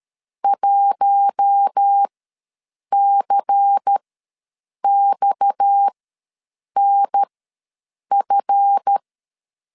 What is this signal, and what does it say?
Morse code, heard as a keyed tone, about 13 wpm, spaced as for 7 wpm, 800 Hz. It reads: 1CXNF
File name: LEZIONI / 7 GF9Z / 7 GF ws 12wpm 3140